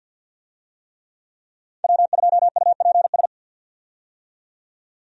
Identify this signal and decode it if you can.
Morse code, heard as a keyed tone, about 50 wpm, 700 Hz. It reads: W2UPS
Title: USA Call Signs - 5 Characters - 50wpm